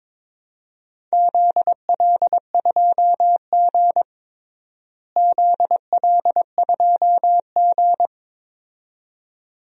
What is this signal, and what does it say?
Morse code, heard as a keyed tone, about 22 wpm, 700 Hz. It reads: ZL2G ZL2G
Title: DXCC-100 Call Signs - 4 Characters - Repeated 2x - 22wpm